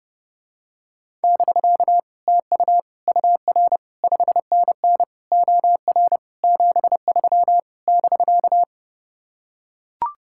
Morse code, heard as a keyed tone, about 29 wpm, 700 Hz. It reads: <BK> TU UR 5NN OR 73 <BK>